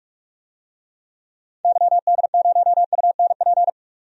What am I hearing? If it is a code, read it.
Morse code, heard as a keyed tone, about 44 wpm, 700 Hz. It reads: YD0UNP
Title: DXCC-100 Call Signs - 6 Characters - 45wpm